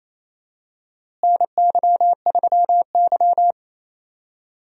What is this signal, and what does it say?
Morse code, heard as a keyed tone, about 27 wpm, 700 Hz. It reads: NY3Y
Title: USA Call Signs - 4 Characters - Repeated 3x - 28wpm